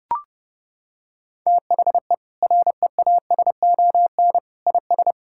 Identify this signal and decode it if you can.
Morse code, heard as a keyed tone, about 30 wpm, 700 Hz. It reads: THE REASON IS
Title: Sentences from Top 1600 Words 30wpm